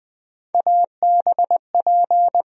Morse code, heard as a keyed tone, about 21 wpm, 700 Hz. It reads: ABP
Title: USA Call Sign Postfixes - 3 Letters 20wpm